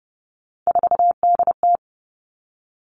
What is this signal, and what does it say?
Morse code, heard as a keyed tone, about 30 wpm, 700 Hz. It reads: 4DT
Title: All Cut Numbers - 3-Digit 30wpm